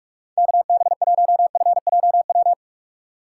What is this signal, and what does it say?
Morse code, heard as a keyed tone, about 46 wpm, 700 Hz. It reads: KB1FJW
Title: USA Call Signs - 3x Character Spacing - 45wpm